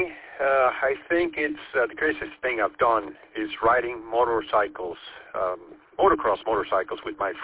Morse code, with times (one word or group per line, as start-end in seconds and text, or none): none